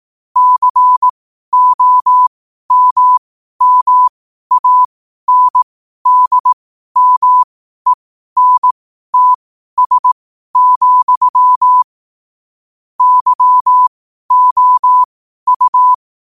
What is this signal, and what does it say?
Morse code, heard as a keyed tone, about 18 wpm, spaced as for 12 wpm, 1000 Hz. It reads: COMMANDMENTS, YOU
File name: qotdfiles / 12wpm